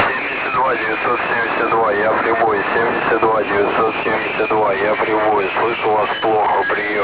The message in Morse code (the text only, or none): none